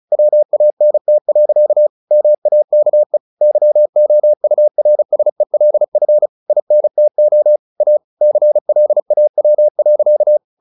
Morse code, heard as a keyed tone, about 35 wpm, 600 Hz. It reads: WANT. MAKE YOURSELF INTO A CLAW.
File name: the old man and the sea / the old man and the sea 1 35wpm